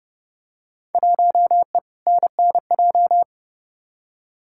Morse code, heard as a keyed tone, about 31 wpm, 700 Hz. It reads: 1E NNJ
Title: Field Day Exchanges - 30wpm